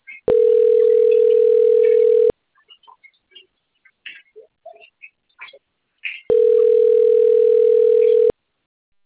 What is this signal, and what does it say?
Ringback tone (North America), 440 + 480 Hz.